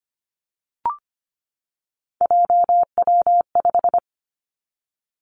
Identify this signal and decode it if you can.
Morse code, heard as a keyed tone, about 25 wpm, 700 Hz. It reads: JW5
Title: Three Challenging Letter-Number Combinations 25wpm